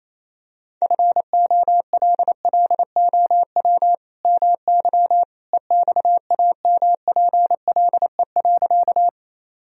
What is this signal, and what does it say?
Morse code, heard as a keyed tone, about 27 wpm, 700 Hz. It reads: FOLLOW MY EXAMPLE.
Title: Sentences from Top 1000 Words - Review Part 1 - 28wpm